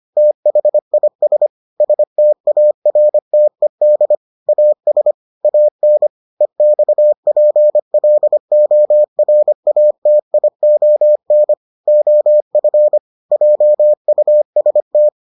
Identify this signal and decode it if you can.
Morse code, heard as a keyed tone, about 25 wpm, 600 Hz. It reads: THIS STARTED AS AN EXPLORATION OF JUST